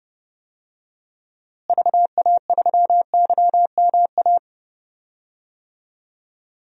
Morse code, heard as a keyed tone, about 30 wpm, 700 Hz. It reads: VA3YMA
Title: DXCC-100 Call Signs - 6 Characters - Repeated 2x - 30wpm